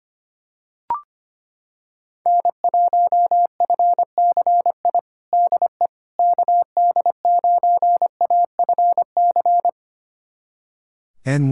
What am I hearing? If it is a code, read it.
Morse code, heard as a keyed tone, about 25 wpm, 700 Hz. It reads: N1FCI DE KD9AFC
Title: USA Callsign DE Callsign 25wpm